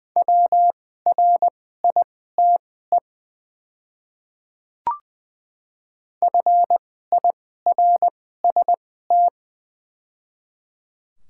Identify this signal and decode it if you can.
Morse code, heard as a keyed tone, about 20 wpm, spaced as for 15 wpm, 700 Hz. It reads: WRITE FIRST
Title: 2x Character Spacing - Top 200 Words - 20wpm